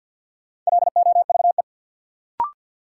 Morse code, heard as a keyed tone, about 47 wpm, 700 Hz. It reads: LOVE
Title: Sentences from Top 400 Words - New Word Drill - 50wpm